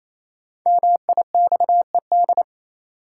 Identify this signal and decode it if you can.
Morse code, heard as a keyed tone, about 28 wpm, 700 Hz. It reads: MIXED
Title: Sentences from Top 1500 Words - New Word Drill - 28wpm